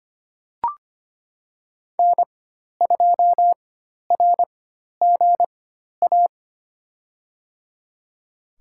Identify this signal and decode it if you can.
Morse code, heard as a keyed tone, about 25 wpm, spaced as for 12 wpm, 700 Hz. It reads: N2RGA
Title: USA Call Signs - 5 Characters - 4x Character Spacing - 25wpm